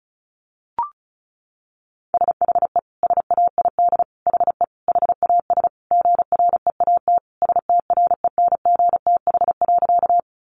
Morse code, heard as a keyed tone, about 34 wpm, 700 Hz. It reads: SHE SAID HE HAS GREAT STRENGTH.
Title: Sentences from Top 1300 Words 35wpm